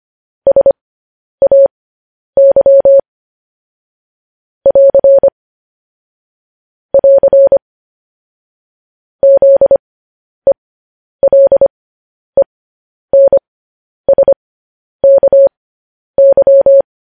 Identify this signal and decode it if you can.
Morse code, heard as a keyed tone, about 25 wpm, spaced as for 10 wpm, 550 Hz. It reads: SAY <AR> <AR> ZELENSKY
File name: podcast / News headlines-1x-25x10